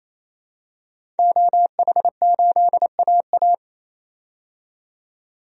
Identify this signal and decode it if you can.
Morse code, heard as a keyed tone, about 28 wpm, 700 Hz. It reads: OH8AA
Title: DXCC-100 Call Signs - 5 Characters - 28wpm